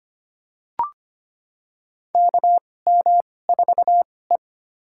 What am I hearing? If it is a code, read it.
Morse code, heard as a keyed tone, about 25 wpm, spaced as for 18 wpm, 700 Hz. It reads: KM4E
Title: USA Call Signs - 2x Character Spacing - 25wpm